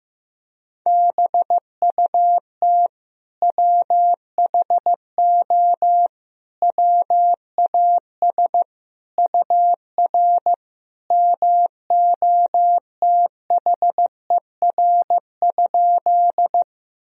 Morse code, heard as a keyed tone, about 15 wpm, 700 Hz. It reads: BUT WHO WAS UR MOTHER?
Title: Sentences from Top 200 Words and Abbreviations 15wpm